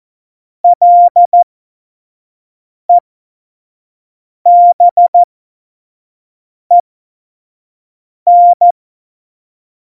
Morse code, heard as a keyed tone, about 14 wpm, spaced as for 5 wpm, 700 Hz. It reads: LEBEN
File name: Morsekurs / Witze / Witz 27. Oktober 2020, 25 F70